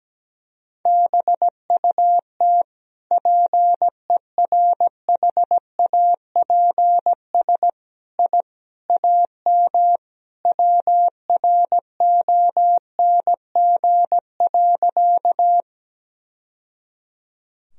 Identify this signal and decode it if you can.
Morse code, heard as a keyed tone, about 17 wpm, 700 Hz. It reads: BUT PERHAPS I AM WRONG.
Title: Sentences from Top 1000 Words - Review Part 2 - 17wpm